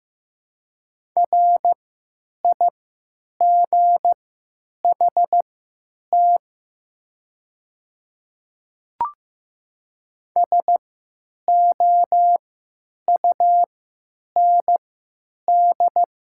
Morse code, heard as a keyed tone, about 15 wpm, spaced as for 9 wpm, 700 Hz. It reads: RIGHT SOUND
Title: 3x Character Spacing - Top 500 Words - 15wpm